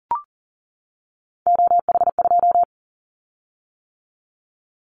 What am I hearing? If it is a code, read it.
Morse code, heard as a keyed tone, about 43 wpm, 700 Hz. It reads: OH2
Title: Callsign Prefixes for Top 50 DXCC - 40wpm